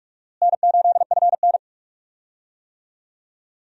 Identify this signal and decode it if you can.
Morse code, heard as a keyed tone, about 46 wpm, 700 Hz. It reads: N8FN
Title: USA Call Signs - 4 Characters - Repeated 2x - 45wpm